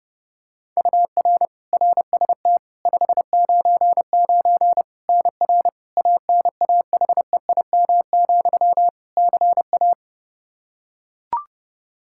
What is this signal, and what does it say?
Morse code, heard as a keyed tone, about 30 wpm, 700 Hz. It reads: UR RST 599 NR ANAHEIM, CA
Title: Practice QSO - Exchange 1 - Beginning - 30wpm